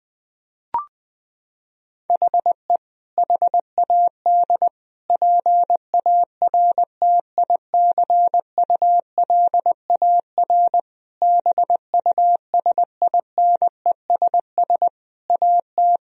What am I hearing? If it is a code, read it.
Morse code, heard as a keyed tone, about 20 wpm, 700 Hz. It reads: HE HAD PARTICULAR BUSINESS AT
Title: Sentences from Top 1000 Words - Review Part 4 - 20wpm